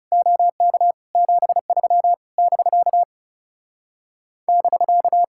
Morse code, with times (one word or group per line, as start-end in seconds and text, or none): 0.1-0.9 OK
1.2-2.1 73
2.4-3.0 <BK>
4.5-5.2 <BK>